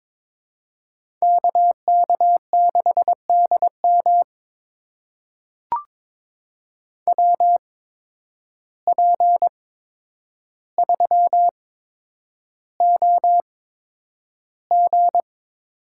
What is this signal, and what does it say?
Morse code, heard as a keyed tone, about 22 wpm, spaced as for 9 wpm, 700 Hz. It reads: KK6DM WP3OG
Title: USA Call Signs - 5 Characters - 8x Character Spacing - 22wpm